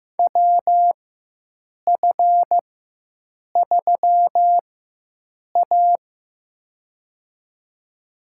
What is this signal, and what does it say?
Morse code, heard as a keyed tone, about 15 wpm, spaced as for 7 wpm, 700 Hz. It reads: WF3A
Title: USA Call Signs - 4x Character Spacing - 15wpm